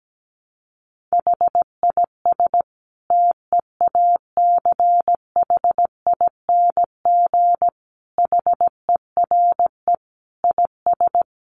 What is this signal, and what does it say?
Morse code, heard as a keyed tone, about 17 wpm, 700 Hz. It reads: HIS TEACHING HERE IS
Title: Sentences from Top 1100 Words 17wpm